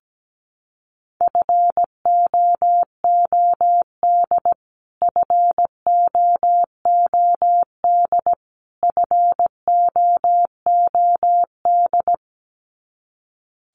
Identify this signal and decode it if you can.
Morse code, heard as a keyed tone, about 17 wpm, 700 Hz. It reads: FOOD FOOD FOOD
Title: Top 500 Words Repeated 17wpm